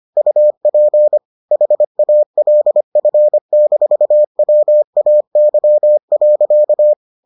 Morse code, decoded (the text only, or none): UP HALF-WAY.